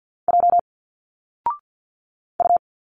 Morse code, heard as a keyed tone, about 50 wpm, 700 Hz. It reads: J U
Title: Single Letters 50wpm